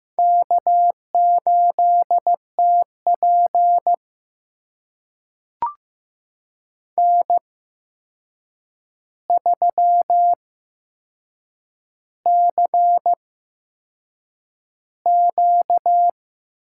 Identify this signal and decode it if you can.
Morse code, heard as a keyed tone, about 15 wpm, spaced as for 6 wpm, 700 Hz. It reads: K8TP N3CQ
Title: USA Call Signs - 8x Character Spacing - 15wpm